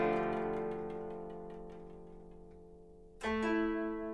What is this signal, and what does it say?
Telephone call progress tone (North America), heard as a telephone network's tone, about 350, 440 Hz, with noise. Dial tone.